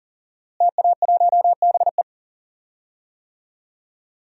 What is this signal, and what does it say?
Morse code, heard as a keyed tone, about 40 wpm, 700 Hz. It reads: TA1BE
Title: DXCC-100 Call Signs - 5 Characters - Repeated 3x - 40wpm